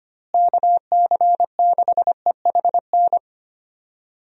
Morse code, heard as a keyed tone, about 25 wpm, 700 Hz. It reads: KC6EHN
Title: USA Call Signs - 6 Characters - Repeated 3x - 25wpm